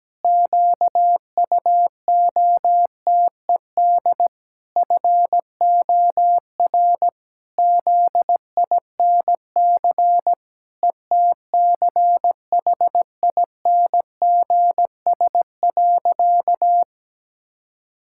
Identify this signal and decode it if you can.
Morse code, heard as a keyed tone, about 17 wpm, 700 Hz. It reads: QUOTED FOR ZINC ETCHINGS.
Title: Pangram Challenge v2 - 17wpm